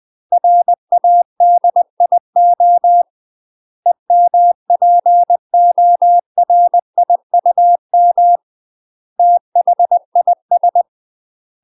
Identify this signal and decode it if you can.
Morse code, heard as a keyed tone, about 20 wpm, 700 Hz. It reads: RADIO EMPORIUM THIS
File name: UK MASTER QSOs TRAINING / 20WPM / UK MASTER QSOs 0030